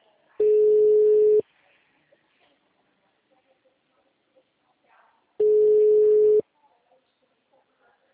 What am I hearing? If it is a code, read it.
Ringback tone (Europe, 425 Hz).
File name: out- -2003- - - .13608